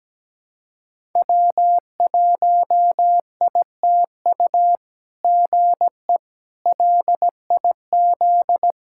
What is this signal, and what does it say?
Morse code, heard as a keyed tone, about 17 wpm, 700 Hz. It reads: W1ITU GE LIZ
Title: SOTA QSO Practice 17wpm